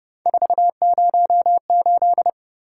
Morse code, heard as a keyed tone, about 30 wpm, 700 Hz. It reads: 408
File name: Speed-Racing - Three Letter-Number Combinations - 25wpm